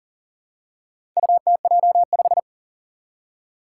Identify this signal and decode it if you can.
Morse code, heard as a keyed tone, about 40 wpm, 700 Hz. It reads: UTJ5
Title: Four Challenging Letter-Number Combinations 40wpm